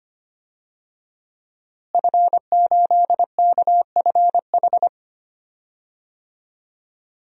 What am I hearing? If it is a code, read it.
Morse code, heard as a keyed tone, about 25 wpm, 700 Hz. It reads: F8KFH
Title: DXCC-100 Call Signs - 5 Characters - Repeated 2x - 25wpm